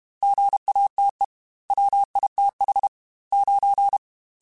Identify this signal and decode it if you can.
Morse code, heard as a keyed tone, about 32 wpm, 800 Hz. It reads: GATE WITH 9
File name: Technical Subjects / Technical Subjects-mp3s / AND Gate @30